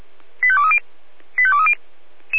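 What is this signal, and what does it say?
Five-tone selective call (ZVEI), heard as quick runs of five tones, tones of about 70 ms. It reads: 85230 85230
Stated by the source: SDR-received ZVEI five-tone call 85360.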